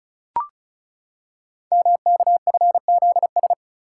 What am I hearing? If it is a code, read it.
Morse code, heard as a keyed tone, about 36 wpm, 690 Hz. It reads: MKFZS
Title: Code Groups - 5 Letters - 35wpm